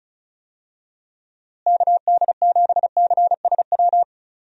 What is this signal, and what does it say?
Morse code, heard as a keyed tone, about 34 wpm, 700 Hz. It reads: KD7CSW
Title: USA Call Signs - 6 Characters - Repeated 3x - 35wpm